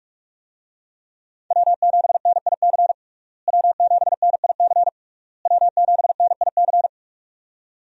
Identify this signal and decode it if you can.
Morse code, heard as a keyed tone, about 41 wpm, 700 Hz. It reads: W7NIC W7NIC W7NIC
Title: USA Call Signs - 5 Characters - Repeated 3x - 45wpm